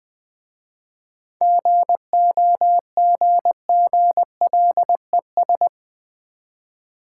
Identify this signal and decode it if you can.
Morse code, heard as a keyed tone, about 20 wpm, 700 Hz. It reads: GOGGLES
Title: Easy Words with Hints - 20wpm